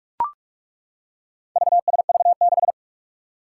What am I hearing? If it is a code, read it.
Morse code, heard as a keyed tone, about 42 wpm, 700 Hz. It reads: VSV6
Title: Four Challenging Letter-Number Combinations 45wpm